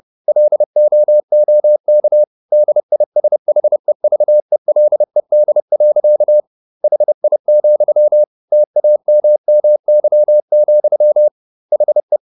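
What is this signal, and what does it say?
Morse code, heard as a keyed tone, about 30 wpm, 600 Hz. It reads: LOOK DISHEVELED. HI, TAMMY, HE
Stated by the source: Morse code at 30 wpm, the novel Never by Ken Follett in English, from CW med Gnister.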